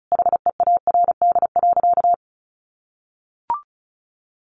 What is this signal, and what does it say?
Morse code, heard as a keyed tone, about 34 wpm, 700 Hz. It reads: HEARD.
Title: Sentences from Top 400 Words 35wpm